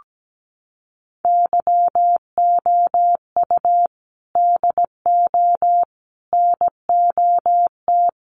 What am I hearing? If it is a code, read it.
Morse code, heard as a keyed tone, about 17 wpm, 700 Hz. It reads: YOU DO NOT
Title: Sentences from Top 1100 Words 17wpm